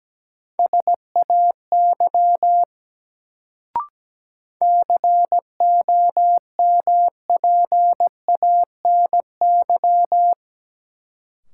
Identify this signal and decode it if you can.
Morse code, heard as a keyed tone, about 17 wpm, 700 Hz. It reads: SAY COMPANY
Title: Sets of 2 Words - New Word Drill - 17wpm